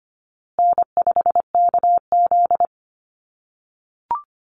Morse code, heard as a keyed tone, about 25 wpm, 700 Hz. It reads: N5KZ
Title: USA Call Signs Repeated 3x 25wpm